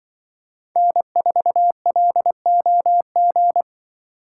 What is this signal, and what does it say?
Morse code, heard as a keyed tone, about 23 wpm, 700 Hz. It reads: N4LOG
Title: Speed Racing - USA 1x3 Call Signs 20wpm